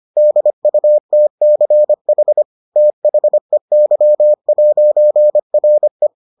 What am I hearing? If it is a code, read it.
Morse code, heard as a keyed tone, about 25 wpm, 600 Hz. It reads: DUTCH THEY'RE